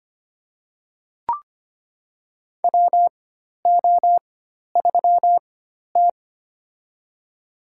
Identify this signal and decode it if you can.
Morse code, heard as a keyed tone, about 25 wpm, spaced as for 12 wpm, 690 Hz. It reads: WO3T
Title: USA Call Signs - 4 Characters - 4x Character Spacing - 25wpm